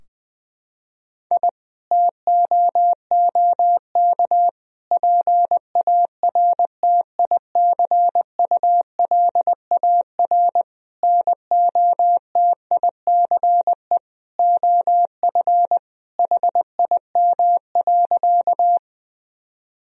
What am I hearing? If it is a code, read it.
Morse code, heard as a keyed tone, about 21 wpm, 700 Hz. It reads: I TOOK PARTICULAR NOTICE OF HIM.
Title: Sentences from Top 1000 Words - Review Part 4 - 20wpm